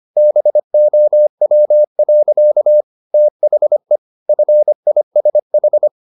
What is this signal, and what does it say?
Morse code, heard as a keyed tone, about 25 wpm, 600 Hz. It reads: BOW. THE FISH